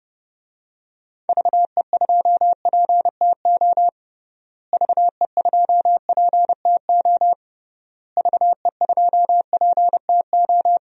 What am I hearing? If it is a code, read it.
Morse code, heard as a keyed tone, about 32 wpm, 700 Hz. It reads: VE2PTO VE2PTO VE2PTO
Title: DXCC-100 Call Signs - 6 Characters - Repeated 3x - 30wpm